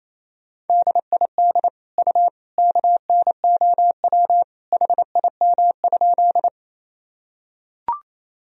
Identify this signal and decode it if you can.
Morse code, heard as a keyed tone, about 28 wpm, 700 Hz. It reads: DID U KNOW HIM?
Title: Sentences from Top 100 Words and Abbreviations 28wpm